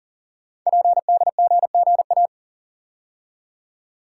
Morse code, heard as a keyed tone, about 40 wpm, 700 Hz. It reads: PDGGA
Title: Code Groups - 5 Letters - 40wpm